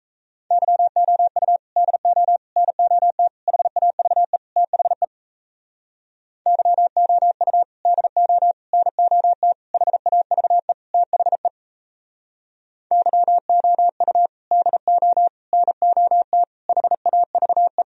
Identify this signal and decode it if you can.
Morse code, heard as a keyed tone, about 37 wpm, 700 Hz. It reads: YOU DO NOT HAVE THE YOU DO NOT HAVE THE YOU DO NOT HAVE